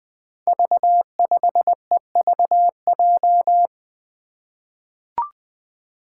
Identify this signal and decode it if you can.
Morse code, heard as a keyed tone, about 20 wpm, 700 Hz. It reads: V5EVJ
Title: Five Challenging Letter-Number Combinations 20wpm